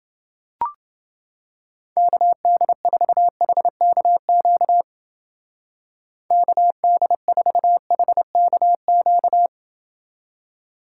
Morse code, heard as a keyed tone, about 28 wpm, 700 Hz. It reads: KD4HKQ KD4HKQ